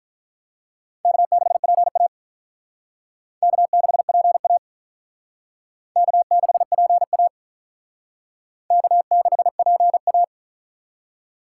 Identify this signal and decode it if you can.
Morse code, heard as a keyed tone, about 47 wpm, 700 Hz. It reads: K6PA K6PA K6PA K6PA